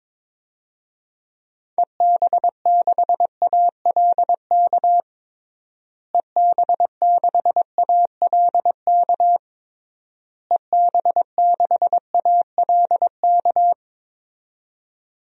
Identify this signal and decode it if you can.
Morse code, heard as a keyed tone, about 22 wpm, 700 Hz. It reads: EB6ALK EB6ALK EB6ALK
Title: DXCC-100 Call Signs - 6 Characters - Repeated 3x - 22wpm